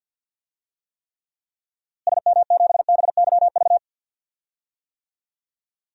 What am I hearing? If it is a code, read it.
Morse code, heard as a keyed tone, about 49 wpm, 700 Hz. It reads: SM7BYV
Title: DXCC-100 Call Signs - 6 Characters - Repeated 3x - 50wpm